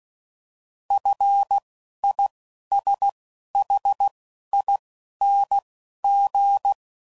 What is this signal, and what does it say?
Morse code, heard as a keyed tone, about 16 wpm, spaced as for 11 wpm, 800 Hz. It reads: FISHING
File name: Prefixes / FLs & THs / F Prefixes